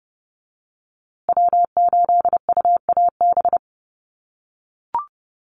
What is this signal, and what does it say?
Morse code, heard as a keyed tone, about 30 wpm, 700 Hz. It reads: W8UAB
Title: USA Call Signs Repeated 2x 30wpm